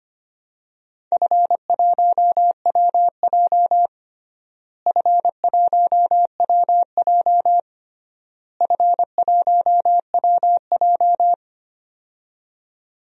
Morse code, heard as a keyed tone, about 25 wpm, 700 Hz. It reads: F1WJ F1WJ F1WJ
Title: DXCC-100 Call Signs - 4 Characters - Repeated 3x - 25wpm